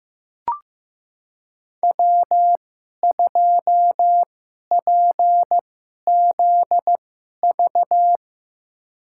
Morse code, heard as a keyed tone, about 15 wpm, spaced as for 11 wpm, 700 Hz. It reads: W2PZV
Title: USA Call Signs - 5 Characters - 2x Character Spacing - 15wpm